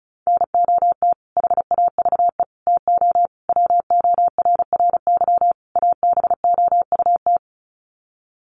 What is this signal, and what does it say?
Morse code, heard as a keyed tone, about 34 wpm, 700 Hz. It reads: NOT HAVE TO WORRY ABOUT
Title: Sets of 5 Words 35wpm